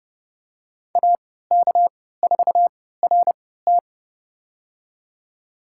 Morse code, heard as a keyed tone, about 29 wpm, spaced as for 17 wpm, 700 Hz. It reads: AK4RT
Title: USA Call Signs - 3x Character Spacing - 30wpm